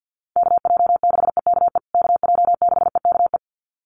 Morse code, heard as a keyed tone, about 50 wpm, 700 Hz. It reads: KJ6EKE KJ6EKE
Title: USA Call Signs Repeated 2x 50wpm